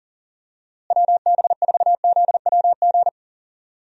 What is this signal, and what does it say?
Morse code, heard as a keyed tone, about 38 wpm, 700 Hz. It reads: WB4ZWG
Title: USA Call Signs - 6 Characters - Repeated 3x - 40wpm